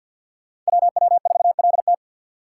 Morse code, heard as a keyed tone, about 62 wpm, 700 Hz. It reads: WW4LT
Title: Speed Racing - USA 2x2 Call Signs 50wpm